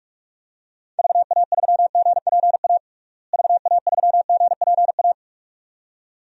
Morse code, heard as a keyed tone, about 47 wpm, 700 Hz. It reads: VA3GPA VA3GPA
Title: DXCC-100 Call Signs - 6 Characters - Repeated 2x - 45wpm